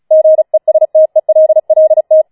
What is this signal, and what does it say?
Morse code, heard as a keyed tone, about 36 wpm, 630 Hz. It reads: GESTELLT
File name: Schneewittchen Teil6 36wpm